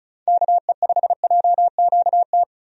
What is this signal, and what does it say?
Morse code, heard as a keyed tone, about 36 wpm, 700 Hz. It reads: KE5JQT
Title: USA Call Signs - 6 Characters - 35wpm